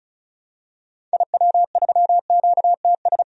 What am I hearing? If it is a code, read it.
Morse code, heard as a keyed tone, about 34 wpm, 700 Hz. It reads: IW3QTS
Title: DXCC-100 Call Signs - 6 Characters - 35wpm